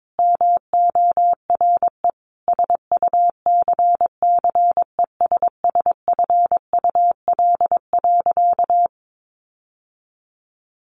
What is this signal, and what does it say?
Morse code, heard as a keyed tone, about 22 wpm, 700 Hz. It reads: MORE SUCCESSFUL.
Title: Sentences from Top 900 Words 22wpm